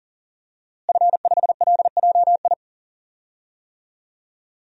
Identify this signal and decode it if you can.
Morse code, heard as a keyed tone, about 40 wpm, 700 Hz. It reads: F5LJI
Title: DXCC-100 Call Signs - 5 Characters - Repeated 3x - 40wpm